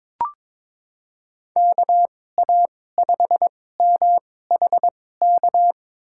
Morse code, heard as a keyed tone, about 22 wpm, spaced as for 16 wpm, 700 Hz. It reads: KA5MHK
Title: USA Call Signs - 6 Characters - 2x Character Spacing - 22wpm